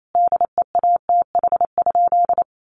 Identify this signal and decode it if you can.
Morse code, heard as a keyed tone, about 28 wpm, 700 Hz. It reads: DEATH?
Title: Sentences from Top 1300 Words 28wpm